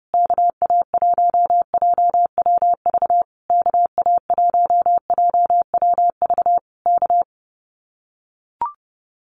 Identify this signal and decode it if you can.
Morse code, heard as a keyed tone, about 29 wpm, 700 Hz. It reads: KA1JWV KA1JWV K
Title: Common QSO Phrases Encore 30wpm